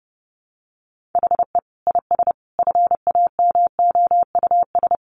Morse code, heard as a keyed tone, about 30 wpm, 700 Hz. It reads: HE IS FAMOUS
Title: Sentences from Top 1500 Words 30wpm